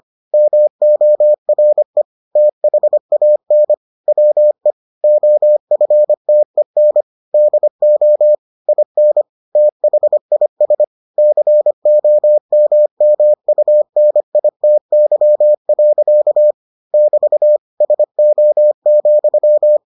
Morse code, heard as a keyed tone, about 25 wpm, 600 Hz. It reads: MORE THAN WE OFTEN DO IN THIS COMMUNITY. <BT> SO,